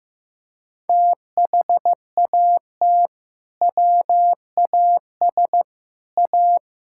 Morse code, heard as a keyed tone, about 15 wpm, 700 Hz. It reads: THAT WAS A
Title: Sentences from Top 1600 Words 15wpm